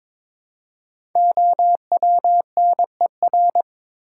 Morse code, heard as a keyed tone, about 22 wpm, 700 Hz. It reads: OWNER
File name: 5 Letter Words - First 3 Letters Given - 22wpm